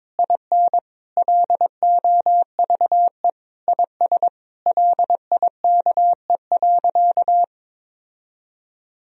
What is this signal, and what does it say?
Morse code, heard as a keyed tone, about 22 wpm, 700 Hz. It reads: IN LOVE IS LIKE.
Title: Sentences from Top 1000 Words - Review Part 3 - 22wpm